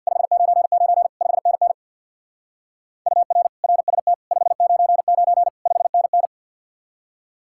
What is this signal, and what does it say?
Morse code, heard as a keyed tone, about 61 wpm, 700 Hz. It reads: 599 5NN UR RST 599 5NN
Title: Speed-Racing Common QSO Phrases Encore - 50wpm